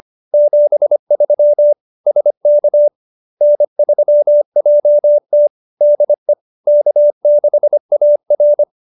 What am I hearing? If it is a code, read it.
Morse code, heard as a keyed tone, about 25 wpm, 570 Hz. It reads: 73 SK N3JT DE K6AR